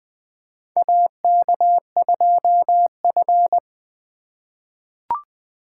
Morse code, heard as a keyed tone, about 20 wpm, 700 Hz. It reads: AK2F
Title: USA Call Signs - 4 Characters - 3x Character Spacing - 20wpm